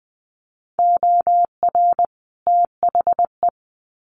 Morse code, heard as a keyed tone, about 20 wpm, 700 Hz. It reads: OR THE
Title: Sets of 2 Words 20wpm